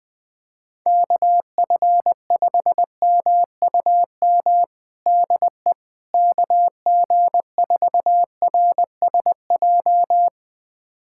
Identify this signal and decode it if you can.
Morse code, heard as a keyed tone, about 21 wpm, 700 Hz. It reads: KF5MUM DE KG4RSJ
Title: USA Callsign DE Callsign - 6 Characters - 20wpm